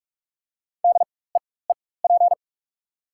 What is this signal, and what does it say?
Morse code, heard as a keyed tone, about 46 wpm, spaced as for 21 wpm, 690 Hz. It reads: DEEP
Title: 4x Character Spacing - Top 500 Words - 45wpm